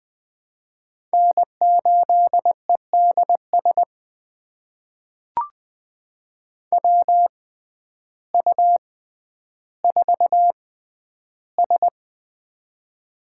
Morse code, heard as a keyed tone, about 20 wpm, spaced as for 11 wpm, 700 Hz. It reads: N8EDS WU4S
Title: USA Call Signs - 6x Character Spacing - 20wpm